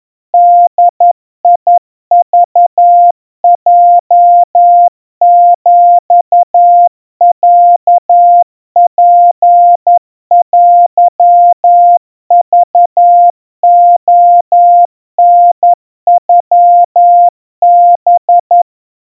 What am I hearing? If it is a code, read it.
Morse code (Japanese Wabun code), heard as a keyed tone, about 11 wpm, 700 Hz. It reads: ボクヲヒロツテクレタノハ
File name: kawa / Morse sample / kamyu 40 cpm 700 Hz Crispy